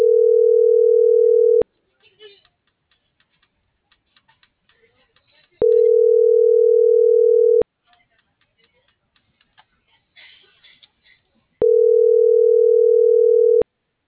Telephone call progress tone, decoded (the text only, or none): ringback tone